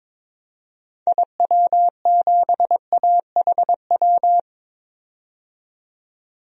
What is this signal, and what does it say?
Morse code, heard as a keyed tone, about 22 wpm, 700 Hz. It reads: IW7AHW
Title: DXCC-100 Call Signs - 6 Characters - 22wpm